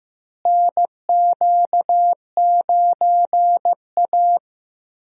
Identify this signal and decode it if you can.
Morse code, heard as a keyed tone, about 15 wpm, 700 Hz. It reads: NQ9A
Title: USA Call Signs - 4 Characters - Repeated 2x - 15wpm